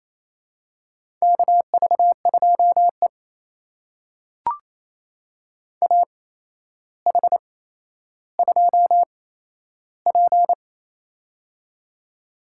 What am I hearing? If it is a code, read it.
Morse code, heard as a keyed tone, about 27 wpm, spaced as for 12 wpm, 700 Hz. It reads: KV2E AH2P
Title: USA Call Signs - 4 Characters - 8x Character Spacing - 28wpm